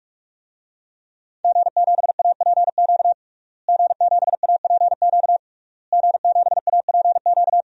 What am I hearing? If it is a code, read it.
Morse code, heard as a keyed tone, about 43 wpm, 700 Hz. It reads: G7APQ G7APQ G7APQ
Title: DXCC-100 Call Signs - 5 Characters - Repeated 3x - 45wpm